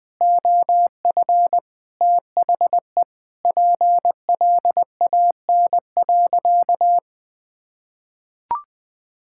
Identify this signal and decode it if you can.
Morse code, heard as a keyed tone, about 21 wpm, 700 Hz. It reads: OF THE PLAN.
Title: Sentences with 8-Word Phrases - 20wpm